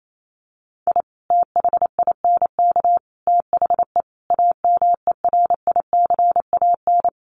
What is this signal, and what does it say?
Morse code, heard as a keyed tone, about 28 wpm, 700 Hz. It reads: I THINK THE AMERICAN
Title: Sets of 5 Words Encore 28wpm